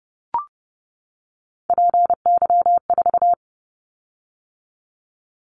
Morse code, heard as a keyed tone, about 30 wpm, 700 Hz. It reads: PY4
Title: Callsign Prefixes for Top 100 DXCC - 30wpm